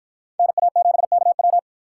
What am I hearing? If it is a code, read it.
Morse code, heard as a keyed tone, about 53 wpm, 700 Hz. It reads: NA7KW